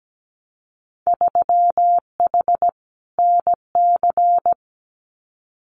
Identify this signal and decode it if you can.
Morse code, heard as a keyed tone, about 17 wpm, 700 Hz. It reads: 3H NC
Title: Winter Field Day Exchanges 17wpm